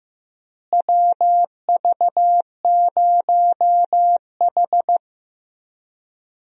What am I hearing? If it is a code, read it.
Morse code, heard as a keyed tone, about 15 wpm, 700 Hz. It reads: WV0H
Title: USA Call Signs - 4 Characters - Repeated 3x - 15wpm